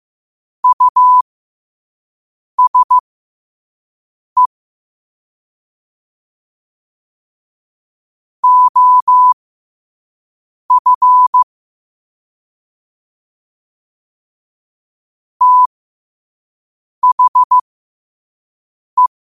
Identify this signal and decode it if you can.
Morse code, heard as a keyed tone, about 15 wpm, spaced as for 5 wpm, 1000 Hz. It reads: USE OF THE
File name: qotdfiles / 5wpm